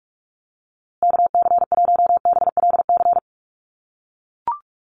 Morse code, heard as a keyed tone, about 44 wpm, 700 Hz. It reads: KC1BLC
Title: USA Call Signs Repeated 3x 45wpm